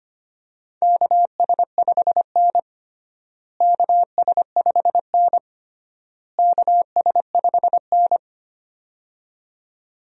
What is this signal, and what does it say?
Morse code, heard as a keyed tone, about 25 wpm, 700 Hz. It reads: KS5N KS5N KS5N
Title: USA Call Signs - 4 Characters - Repeated 3x - 25wpm